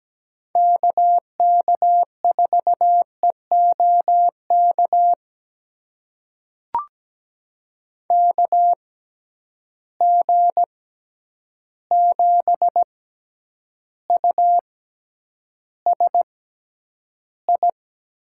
Morse code, heard as a keyed tone, about 17 wpm, spaced as for 9 wpm, 700 Hz. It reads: KK4EOK KG7USI
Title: USA Call Signs - 6x Character Spacing - 17wpm